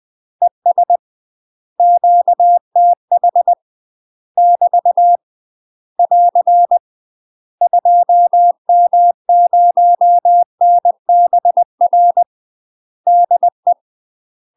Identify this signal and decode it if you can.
Morse code, heard as a keyed tone, about 20 wpm, 700 Hz. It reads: ES QTH <BT> <AR> 2M0NBR DE
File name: UK MASTER QSOs TRAINING / 20WPM / UK MASTER QSOs 0014